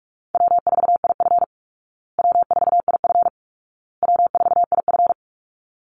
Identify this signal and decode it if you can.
Morse code, heard as a keyed tone, about 44 wpm, 700 Hz. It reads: W4IF W4IF W4IF
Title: Call Signs Repeated 2x - 45wpm